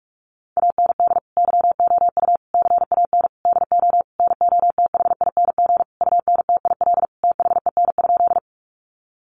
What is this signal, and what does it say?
Morse code, heard as a keyed tone, about 44 wpm, 700 Hz. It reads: AND YOU CAN DO NOTHING UNTIL THEN?
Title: Sentences from Top 500 Words 45wpm